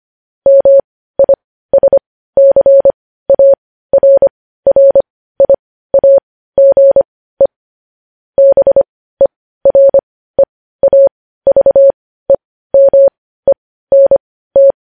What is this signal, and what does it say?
Morse code, heard as a keyed tone, about 25 wpm, spaced as for 15 wpm, 550 Hz. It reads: MISCARRIAGE BEREAVEMENT
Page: News Headlines in Morse Code at 15 WPM